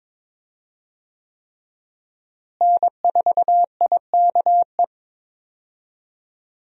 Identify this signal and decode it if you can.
Morse code, heard as a keyed tone, about 22 wpm, 700 Hz. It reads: N4IKE
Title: USA Call Signs - 5 Characters - Repeated 3x - 22wpm